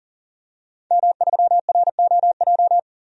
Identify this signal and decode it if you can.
Morse code, heard as a keyed tone, about 39 wpm, 700 Hz. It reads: M3ROJ
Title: DXCC-100 Call Signs - 5 Characters - 40wpm